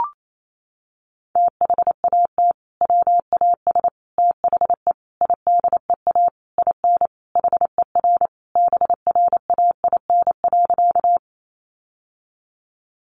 Morse code, heard as a keyed tone, about 28 wpm, 700 Hz. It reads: THAT WAS THE IDEA IN HER BRAIN.
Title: Sentences from Top 1100 Words 28wpm